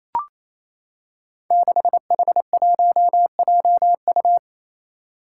Morse code, heard as a keyed tone, about 28 wpm, 700 Hz. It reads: 6H1JU
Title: Five Challenging Letter-Number Combinations 28wpm